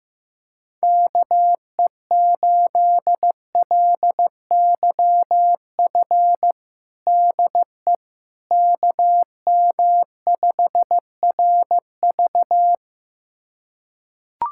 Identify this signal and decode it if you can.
Morse code, heard as a keyed tone, about 15 wpm, 700 Hz. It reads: KE8LYF DE KM5RV
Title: USA Callsign DE Callsign 15wpm